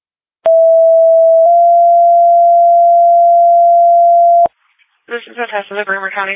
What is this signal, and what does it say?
Two-tone page, heard as a one-second tone, then a three-second tone, about 660 then 690 Hz, with speech. It is heard from 0.5 to 4.5 s.